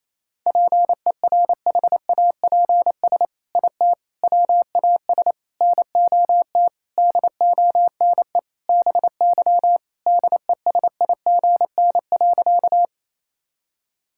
Morse code, heard as a keyed tone, about 28 wpm, 700 Hz. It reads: PERHAPS IT WAS NOT DONE BY DESIGN.
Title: Sentences from Top 1000 Words - Review Part 5 - 28wpm